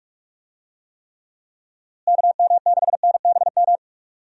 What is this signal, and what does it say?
Morse code, heard as a keyed tone, about 46 wpm, 700 Hz. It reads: KM6NBM
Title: USA Call Signs - 6 Characters - Repeated 2x - 45wpm